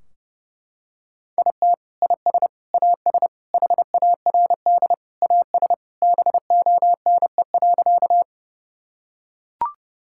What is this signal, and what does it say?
Morse code, heard as a keyed tone, about 30 wpm, 700 Hz. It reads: IT IS AS HARD AS BONE.
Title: Sentences from Top 1600 Words 30wpm